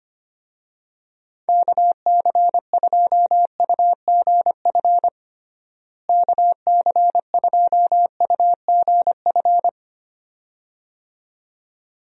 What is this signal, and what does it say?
Morse code, heard as a keyed tone, about 25 wpm, 700 Hz. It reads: KC2UGF KC2UGF
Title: USA Call Signs - 6 Characters - Repeated 2x - 25wpm